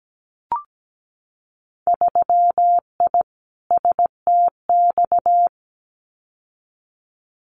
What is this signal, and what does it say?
Morse code, heard as a keyed tone, about 17 wpm, 700 Hz. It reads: 3I STX
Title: Winter Field Day Exchanges 17wpm